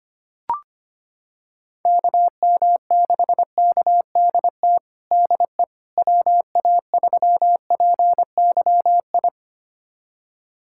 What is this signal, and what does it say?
Morse code, heard as a keyed tone, about 25 wpm, 700 Hz. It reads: KM6KDT DE WA3PYI
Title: USA Callsign DE Callsign - 6 Characters - 25wpm